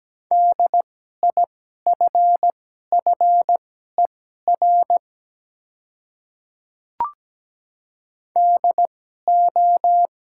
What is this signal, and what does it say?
Morse code, heard as a keyed tone, about 17 wpm, spaced as for 12 wpm, 700 Hz. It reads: DIFFER DO
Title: 2x Character Spacing - Top 300 Words - 17wpm